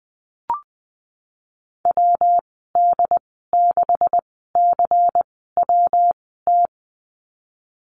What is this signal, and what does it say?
Morse code, heard as a keyed tone, about 20 wpm, 700 Hz. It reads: W D 6 C W T
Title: 154 Lesson 25g - Letter p - Call Signs 20wpm